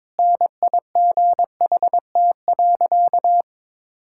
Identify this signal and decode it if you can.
Morse code, heard as a keyed tone, about 22 wpm, 700 Hz. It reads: NIGHT.
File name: practice / 22wpm / Easy Sentences with Hints - 22wpm